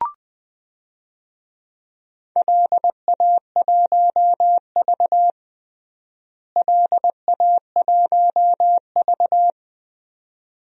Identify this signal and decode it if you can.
Morse code, heard as a keyed tone, about 20 wpm, 700 Hz. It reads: LA1V LA1V
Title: DXCC-100 Call Signs - 4 Characters - Repeated 2x - 20wpm